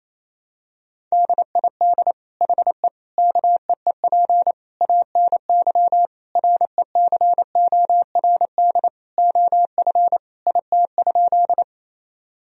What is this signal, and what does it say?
Morse code, heard as a keyed tone, about 28 wpm, 700 Hz. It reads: DID HE KEEP ANY RECORD OF IT?
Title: Sentences from Top 1000 Words - Review Part 3 - 28wpm